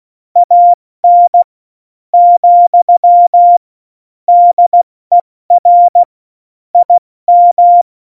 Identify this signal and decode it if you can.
Morse code, heard as a keyed tone, about 16 wpm, 700 Hz. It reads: AN , DER IM
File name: Morsekurs / Witze / Witz 29. Mai 2018, 70 F80